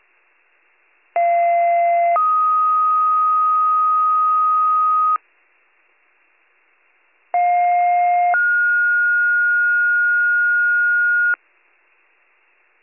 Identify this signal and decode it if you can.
Two-tone page, heard as a one-second tone, then a three-second tone, about 690 then 1250 Hz, about 710 then 1430 Hz. It is heard from 1.2 to 11.3 s.